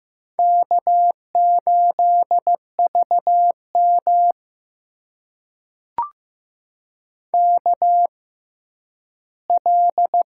Morse code, heard as a keyed tone, about 15 wpm, spaced as for 10 wpm, 700 Hz. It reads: K8VM KL
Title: USA Call Signs - 4 Characters - 6x Character Spacing - 15wpm